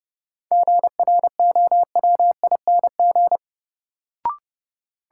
Morse code, heard as a keyed tone, about 30 wpm, 700 Hz. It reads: GROWING
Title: Sets of 5 Words Encore - New Word Drill - 30wpm